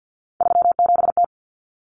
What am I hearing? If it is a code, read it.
Morse code, heard as a keyed tone, about 50 wpm, 700 Hz. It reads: 37T
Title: Common Cut Numbers - 3-Digit 50wpm